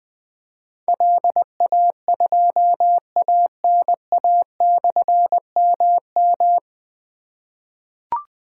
Morse code, heard as a keyed tone, about 20 wpm, 700 Hz. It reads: LA2ANA/MM
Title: Difficult Call Signs 20wpm